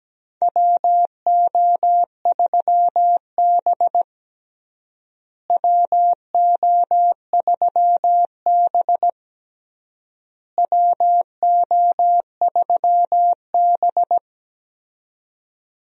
Morse code, heard as a keyed tone, about 17 wpm, 700 Hz. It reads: WO3B WO3B WO3B
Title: USA Call Signs - 4 Characters - Repeated 3x - 17wpm